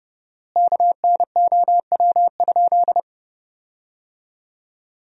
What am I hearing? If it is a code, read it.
Morse code, heard as a keyed tone, about 30 wpm, 700 Hz. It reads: KNOW?
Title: 4x Word-Spacing - Sentences from Top 100 Words - 30wpm